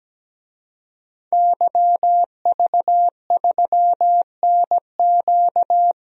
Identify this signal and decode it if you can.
Morse code, heard as a keyed tone, about 17 wpm, 700 Hz. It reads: YV3NQ
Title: DXCC-100 Call Signs - 5 Characters - 17wpm